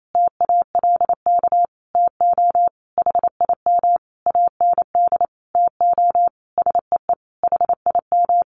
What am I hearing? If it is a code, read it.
Morse code, heard as a keyed tone, about 28 wpm, 700 Hz. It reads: TALK TO HIM AND TO SEE HIM
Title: Sentences from Top 1400 Words 28wpm